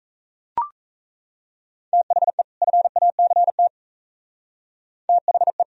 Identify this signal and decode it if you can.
Morse code, heard as a keyed tone, about 33 wpm, 700 Hz. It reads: THE FACT THE
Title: Speed-Racing - Sets of 2 Words 28wpm